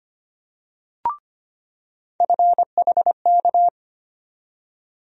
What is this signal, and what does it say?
Morse code, heard as a keyed tone, about 25 wpm, 700 Hz. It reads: FHK
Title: USA Call Sign Postfixes - 3 Letters 25wpm